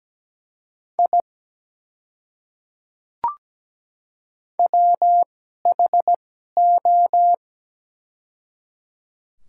Morse code, heard as a keyed tone, about 17 wpm, spaced as for 12 wpm, 700 Hz. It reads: I WHO
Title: 2x Character Spacing - Top 200 Words - 17wpm